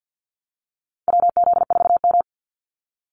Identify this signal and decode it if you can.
Morse code, heard as a keyed tone, about 50 wpm, 700 Hz. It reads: WZ4M
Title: NA SOTA Activator Callsigns 50wpm